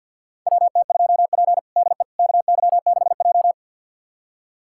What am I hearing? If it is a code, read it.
Morse code, heard as a keyed tone, about 52 wpm, 700 Hz. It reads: WT2P DE KY6J
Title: USA Callsign DE Callsign - 4 Characters - 50wpm